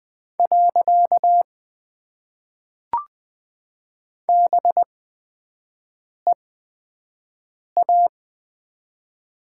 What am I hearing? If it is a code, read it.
Morse code, heard as a keyed tone, about 20 wpm, spaced as for 5 wpm, 700 Hz. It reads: . BEA
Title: Top 90 QSO Elements - 8x Character Spacing - 20wpm